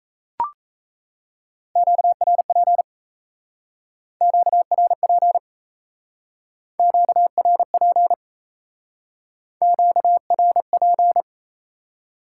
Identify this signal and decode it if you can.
Morse code, heard as a keyed tone, about 36 wpm, 700 Hz. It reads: QRP QRP QRP QRP